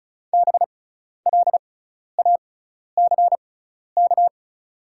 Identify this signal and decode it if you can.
Morse code, heard as a keyed tone, about 36 wpm, spaced as for 12 wpm, 700 Hz. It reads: BLACK
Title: 6x Character Spacing - Top 500 Words - 35wpm